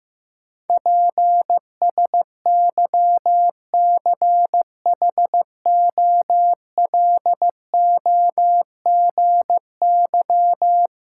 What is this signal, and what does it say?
Morse code, heard as a keyed tone, about 15 wpm, 700 Hz. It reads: PSYCHOLOGY
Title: Sentences from Top 1400 Words - New Word Drill Alt - 15wpm